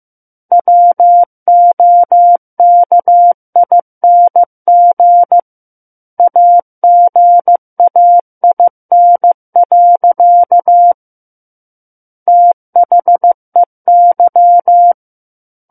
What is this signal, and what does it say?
Morse code, heard as a keyed tone, about 15 wpm, 700 Hz. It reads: WOKING AGAIN. THEY
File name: Book One / 06-Chapter 6 - 15 WPM